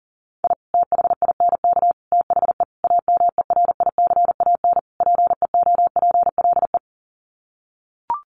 Morse code, heard as a keyed tone, about 37 wpm, 700 Hz. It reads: I THINK THE AMERICAN PEOPLE
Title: Sets of 5 Words Encore 40wpm